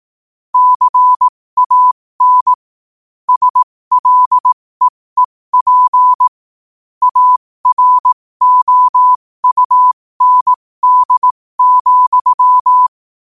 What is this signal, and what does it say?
Morse code, heard as a keyed tone, about 18 wpm, spaced as for 15 wpm, 1000 Hz. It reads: CAN SLEEP AROUND,
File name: qotdfiles / 15wpm